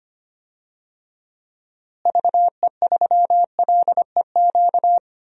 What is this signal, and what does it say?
Morse code, heard as a keyed tone, about 25 wpm, 700 Hz. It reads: VE3LEQ
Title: DXCC-100 Call Signs - 6 Characters - Repeated 3x - 25wpm